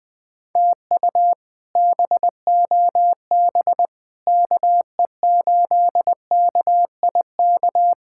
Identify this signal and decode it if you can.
Morse code, heard as a keyed tone, about 20 wpm, 700 Hz. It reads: TU BOB KE8KIK
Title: K1USN SST QSO Practice - 20wpm